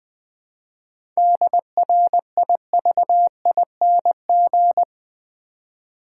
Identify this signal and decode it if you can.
Morse code, heard as a keyed tone, about 20 wpm, 700 Hz. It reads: DRIVING
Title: 7 Letter Words - First 3 Letters Given - 20wpm